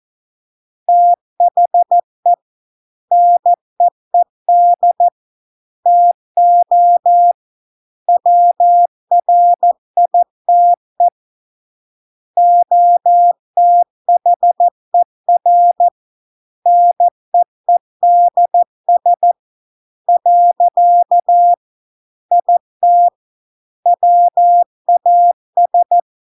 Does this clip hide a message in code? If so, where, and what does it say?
Yes: Morse code from 0.9 to 26.0 s, heard as a keyed tone, about 14 wpm, 700 Hz. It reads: THE NEED TO WRITE OTHER NEEDS . IT WAS